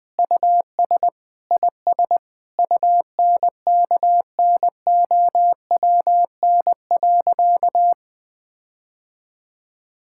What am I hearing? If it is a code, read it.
Morse code, heard as a keyed tone, about 20 wpm, 700 Hz. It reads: US IS UNKNOWN.
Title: Sentences with 4-Word Phrases - Spoken Beginning - 20wpm